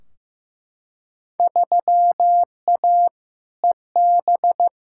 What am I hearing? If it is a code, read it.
Morse code, heard as a keyed tone, about 15 wpm, 700 Hz. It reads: 3A EB
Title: Field Day Exchanges Encore - 15wpm